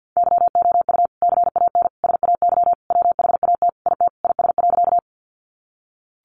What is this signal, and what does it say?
Morse code, heard as a keyed tone, about 50 wpm, 700 Hz. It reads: YOU CAN SAY WHAT IT IS.
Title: Sentences from Top 200 Words 50wpm